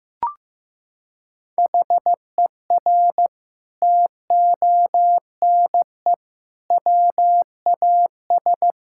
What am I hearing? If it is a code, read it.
Morse code, heard as a keyed tone, about 15 wpm, 700 Hz. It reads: HER TONE WAS
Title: Sentences from Top 1600 Words 15wpm